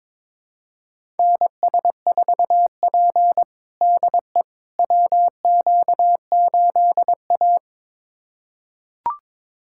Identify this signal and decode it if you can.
Morse code, heard as a keyed tone, about 22 wpm, 700 Hz. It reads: NS4P DE WQ8A
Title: USA Callsign DE Callsign - 4 Characters - 22wpm